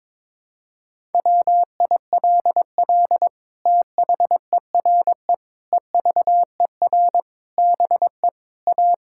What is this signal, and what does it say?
Morse code, heard as a keyed tone, about 22 wpm, 700 Hz. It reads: WILL THERE EVER BE A
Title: Sentences from Top 1000 Words - Review Part 5 - 22wpm